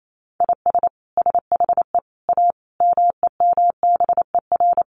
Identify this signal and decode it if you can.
Morse code, heard as a keyed tone, about 28 wpm, 700 Hz. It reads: IS SHE A MEMBER
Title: Sentences from Top 1100 Words 28wpm